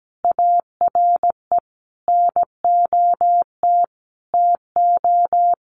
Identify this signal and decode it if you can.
Morse code, heard as a keyed tone, about 17 wpm, 700 Hz. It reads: ARE NOT TO
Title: Sentences from Top 1200 Words 17wpm